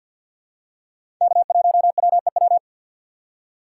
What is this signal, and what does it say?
Morse code, heard as a keyed tone, about 48 wpm, 700 Hz. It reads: K1WEW
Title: USA Call Signs - 5 Characters - 6x Character Spacing - 50wpm